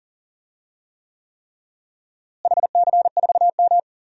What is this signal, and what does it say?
Morse code, heard as a keyed tone, about 40 wpm, 700 Hz. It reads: HC4M
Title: DXCC-100 Call Signs - 4 Characters - Repeated 3x - 40wpm